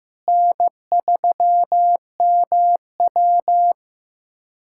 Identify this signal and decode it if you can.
Morse code, heard as a keyed tone, about 15 wpm, 700 Hz. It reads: N3MW
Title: USA Callsign DE Callsign - 4 Characters - 15wpm